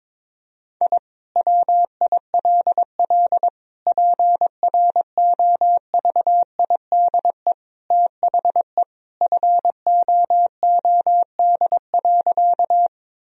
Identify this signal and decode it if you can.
Morse code, heard as a keyed tone, about 22 wpm, 700 Hz. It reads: I WILL PROVIDE THE FOOD.
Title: Sentences from Top 1000 Words - Review Part 3 - 22wpm